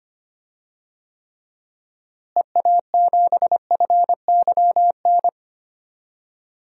Morse code, heard as a keyed tone, about 25 wpm, 700 Hz. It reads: EA7FYN
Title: DXCC-100 Call Signs - 6 Characters - Repeated 2x - 25wpm